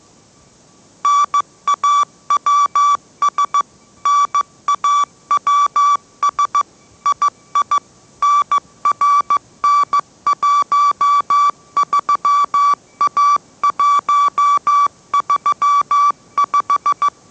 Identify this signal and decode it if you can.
Morse code, heard as a keyed tone, about 15 wpm, 1150 Hz. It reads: NAWS NAWS II NRN13A135
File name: mrx / mrx065 16wpm